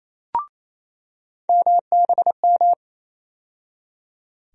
Morse code, heard as a keyed tone, about 28 wpm, 700 Hz. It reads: MBM
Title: USA Call Sign Postfixes - 3 Letters 28wpm